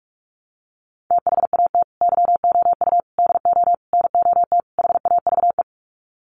Morse code, heard as a keyed tone, about 44 wpm, 700 Hz. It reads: THAT YOU DO NOT HAVE
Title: Sets of 5 Words Encore 45wpm